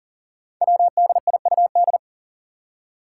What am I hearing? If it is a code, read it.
Morse code, heard as a keyed tone, about 38 wpm, 700 Hz. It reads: WDIUD
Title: Code Groups - 5 Letter:Numbers - 40wpm